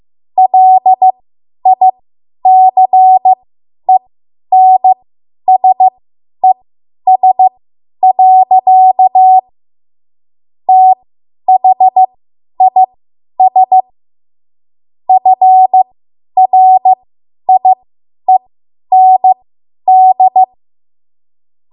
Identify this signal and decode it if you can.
Morse code, heard as a keyed tone, about 15 wpm, spaced as for 10 wpm, 750 Hz. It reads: LICENSES. THIS FRIEND
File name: Morse / 10WPM